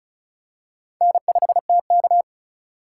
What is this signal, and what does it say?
Morse code, heard as a keyed tone, about 34 wpm, 700 Hz. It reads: N5TK
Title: USA Call Signs - 4 Characters - 2x Character Spacing - 35wpm